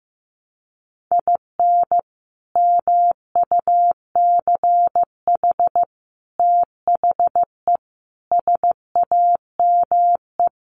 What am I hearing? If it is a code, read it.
Morse code, heard as a keyed tone, about 15 wpm, 700 Hz. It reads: IN MUCH THE SAME
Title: Sets of 5 Words Encore 15wpm